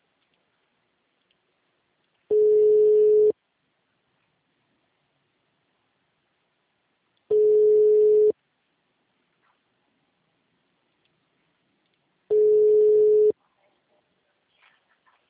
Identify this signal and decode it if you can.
Ringback tone (Europe, 425 Hz).